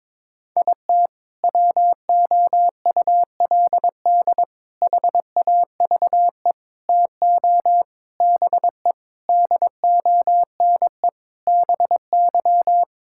Morse code, heard as a keyed tone, about 22 wpm, 700 Hz. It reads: IT WOULD HAVE TO BE DONE BY
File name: practice / 22wpm / Sentences with 4-Word Phrases - 22wpm